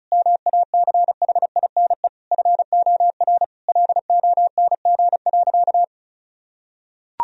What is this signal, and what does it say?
Morse code, heard as a keyed tone, about 34 wpm, 700 Hz. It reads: MACHINE FOR LONG.
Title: Sentences from Top 1000 Words - Review Part 2 - 35wpm